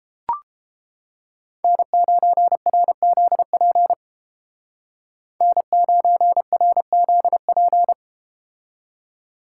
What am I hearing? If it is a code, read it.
Morse code, heard as a keyed tone, about 31 wpm, 700 Hz. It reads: N9RZP N9RZP